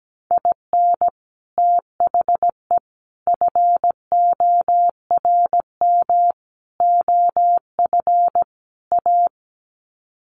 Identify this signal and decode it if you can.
Morse code, heard as a keyed tone, about 17 wpm, 700 Hz. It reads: IN THE FORM OF A
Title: Sets of 5 Words 17wpm